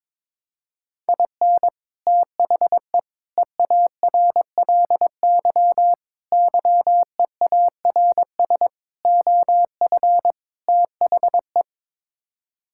Morse code, heard as a keyed tone, about 21 wpm, 700 Hz. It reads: IN THE EARLY YEARS OF THE
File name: practice / 22wpm / Sets of 6 Words - 22wpm